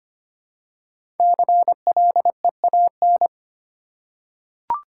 Morse code, heard as a keyed tone, about 25 wpm, 700 Hz. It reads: CLEAN
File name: 5 Letter Words - First 3 Letters Given - 25wpm